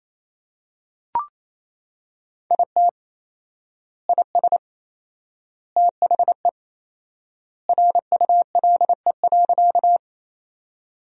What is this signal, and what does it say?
Morse code, heard as a keyed tone, about 28 wpm, 700 Hz. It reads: IT IS THE RULE.
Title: 4x Word-Spacing - Sentences from Top 400 Words - 28wpm